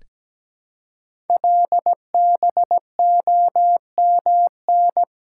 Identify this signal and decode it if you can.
Morse code, heard as a keyed tone, about 17 wpm, 700 Hz. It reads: LBOMN
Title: Code Groups - 5 Letters - 17wpm